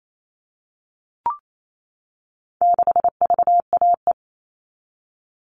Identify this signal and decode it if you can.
Morse code, heard as a keyed tone, about 28 wpm, 700 Hz. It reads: 6VAE
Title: All Cut Numbers - 4-Digit 28wpm